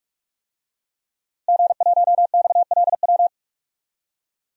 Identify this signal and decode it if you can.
Morse code, heard as a keyed tone, about 46 wpm, 700 Hz. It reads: G1XLW
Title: DXCC-100 Call Signs - 5 Characters - Repeated 2x - 45wpm